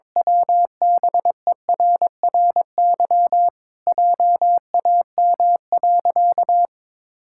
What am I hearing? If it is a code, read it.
Morse code, heard as a keyed tone, about 22 wpm, 700 Hz. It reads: WBERRY JAM.
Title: Pangram Challenge v2 - 22wpm